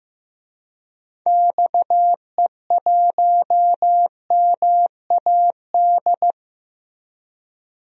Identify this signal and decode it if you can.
Morse code, heard as a keyed tone, about 15 wpm, 700 Hz. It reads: XE1MAD
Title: DXCC-100 Call Signs - 6 Characters - 15wpm